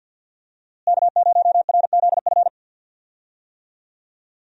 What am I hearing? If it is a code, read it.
Morse code, heard as a keyed tone, about 53 wpm, 700 Hz. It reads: K0RZF
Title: USA Call Signs - 5 Characters - Repeated 2x - 50wpm